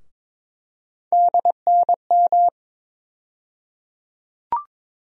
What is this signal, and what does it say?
Morse code, heard as a keyed tone, about 22 wpm, 700 Hz. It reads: DNM
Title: USA Call Sign Postfixes - 3 Letters 22wpm